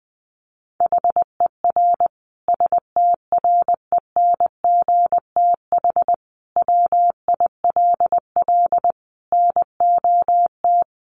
Morse code, heard as a keyed tone, about 20 wpm, 700 Hz. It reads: HER STRENGTH WILL NOT
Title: Sentences from Top 1300 Words 20wpm